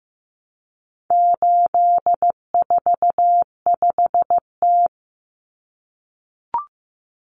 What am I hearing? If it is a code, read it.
Morse code, heard as a keyed tone, about 15 wpm, 700 Hz. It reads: 845T
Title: Common Cut Numbers - 4-Digit 15wpm